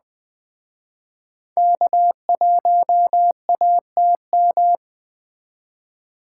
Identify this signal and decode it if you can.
Morse code, heard as a keyed tone, about 20 wpm, 700 Hz. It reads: K1ATM
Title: Speed Racing - USA 1x3 Call Signs 17wpm